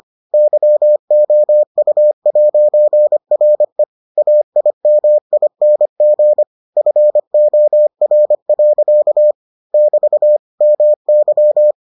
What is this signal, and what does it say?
Morse code, heard as a keyed tone, about 25 wpm, 600 Hz. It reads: YOU'RE AIMING FOR. <BT> MY